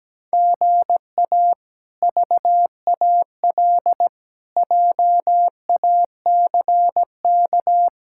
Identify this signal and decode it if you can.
Morse code, heard as a keyed tone, about 17 wpm, 700 Hz. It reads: GA VAL JACK
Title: K1USN SST QSO Practice - 17wpm